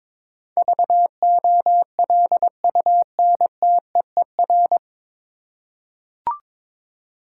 Morse code, heard as a keyed tone, about 22 wpm, 700 Hz. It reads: VOLUNTEER
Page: Morse Code Ninja practice 22wpm